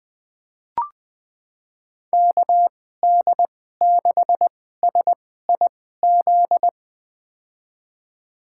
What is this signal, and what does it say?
Morse code, heard as a keyed tone, about 20 wpm, spaced as for 14 wpm, 700 Hz. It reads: KD6SIZ
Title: USA Call Signs - 6 Characters - 2x Character Spacing - 20wpm